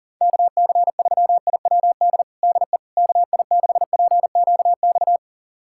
Morse code, heard as a keyed tone, about 40 wpm, 700 Hz. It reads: KC3IWD DE KI6PQX
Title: USA Callsign DE Callsign - 6 Characters - 40wpm